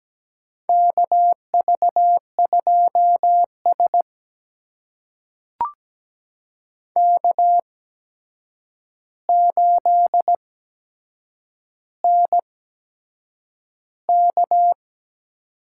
Morse code, heard as a keyed tone, about 17 wpm, spaced as for 7 wpm, 700 Hz. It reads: KV2S K8NK
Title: USA Call Signs - 4 Characters - 8x Character Spacing - 17wpm